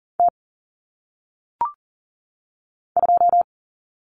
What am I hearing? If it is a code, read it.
Morse code, heard as a keyed tone, about 40 wpm, 700 Hz. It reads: T 2
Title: Common Cut Numbers - 1-Digit 40wpm